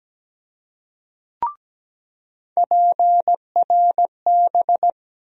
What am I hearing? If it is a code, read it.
Morse code, heard as a keyed tone, about 17 wpm, 700 Hz. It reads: PRB
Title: USA Call Sign Postfixes - 3 Letters 17wpm